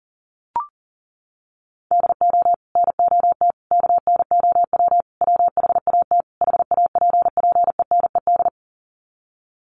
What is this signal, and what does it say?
Morse code, heard as a keyed tone, about 40 wpm, 700 Hz. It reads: DO NOT KNOW WHAT HAPPENED
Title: Sets of 5 Words Encore2 40wpm